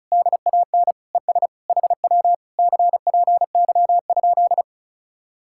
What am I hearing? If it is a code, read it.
Morse code, heard as a keyed tone, about 45 wpm, 700 Hz. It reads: DAN ES HW CPY?